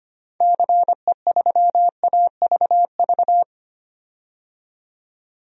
Morse code, heard as a keyed tone, about 25 wpm, 700 Hz. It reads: CE3AVV
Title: DXCC-100 Call Signs - 6 Characters - 25wpm